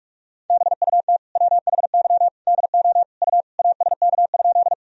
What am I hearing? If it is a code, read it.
Morse code, heard as a keyed tone, about 46 wpm, 700 Hz. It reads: BUT WHY DO U ASK?
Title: Sentences from Top 200 Words and Abbreviations 45wpm